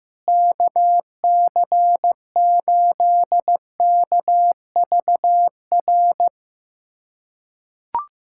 Morse code, heard as a keyed tone, about 15 wpm, 700 Hz. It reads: KC8KVR
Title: USA Callsign DE Callsign - 6 Characters - 15wpm